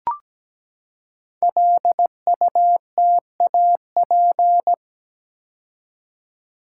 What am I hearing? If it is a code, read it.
Morse code, heard as a keyed tone, about 17 wpm, 700 Hz. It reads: LUTAP
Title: Code Groups - 5 Letters - 17wpm